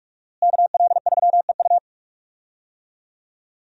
Morse code, heard as a keyed tone, about 45 wpm, 700 Hz. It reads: KL3EU
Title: USA Call Signs - 5 Characters - Repeated 3x - 45wpm